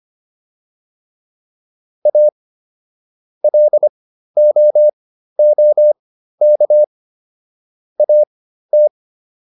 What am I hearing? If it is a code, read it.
Morse code, heard as a keyed tone, about 25 wpm, spaced as for 13 wpm, 600 Hz. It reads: A LOOK AT